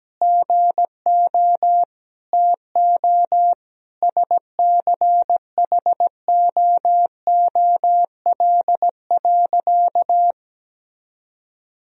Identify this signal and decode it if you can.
Morse code, heard as a keyed tone, about 17 wpm, 700 Hz. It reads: GO TO SCHOOL.
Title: Sentences from Top 1000 Words - Review Part 2 - 17wpm